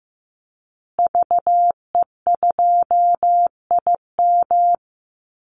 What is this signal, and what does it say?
Morse code, heard as a keyed tone, about 15 wpm, 700 Hz. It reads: VE2IM
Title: Call Signs Repeated 2x - 15wpm